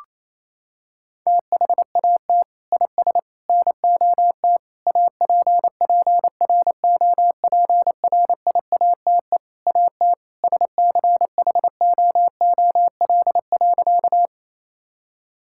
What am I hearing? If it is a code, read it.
Morse code, heard as a keyed tone, about 28 wpm, 700 Hz. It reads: THAT IS NOT APPROPRIATE AT SCHOOL.
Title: Sentences from Top 1000 Words - Review Part 5 - 28wpm